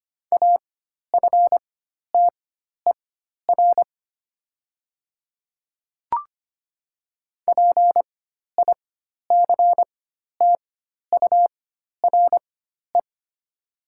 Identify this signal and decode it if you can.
Morse code, heard as a keyed tone, about 25 wpm, spaced as for 12 wpm, 700 Hz. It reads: AFTER PICTURE
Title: 4x Character Spacing - Top 300 Words - 25wpm